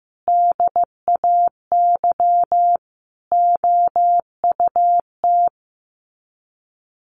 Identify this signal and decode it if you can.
Morse code, heard as a keyed tone, about 15 wpm, 700 Hz. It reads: DAY OUT
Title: Sets of 5 Words Encore2 15wpm